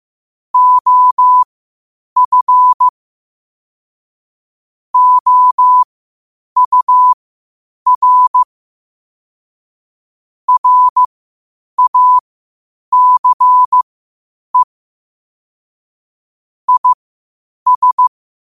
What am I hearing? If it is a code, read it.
Morse code, heard as a keyed tone, about 15 wpm, spaced as for 8 wpm, 1000 Hz. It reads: OF OUR RACE IS
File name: qotdfiles / 8wpm